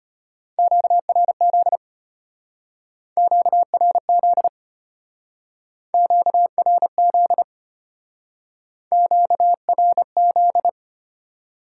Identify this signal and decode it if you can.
Morse code, heard as a keyed tone, about 31 wpm, 700 Hz. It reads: QRZ QRZ QRZ QRZ
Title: Speed-Racing Top 90 QSO Elements 25wpm